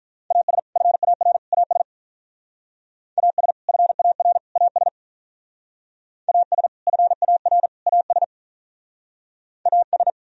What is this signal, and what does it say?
Morse code, heard as a keyed tone, about 44 wpm, 700 Hz. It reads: AS FAR AS AS FAR AS AS FAR AS AS
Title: Speed-Racing - Sets of 3 Words 35wpm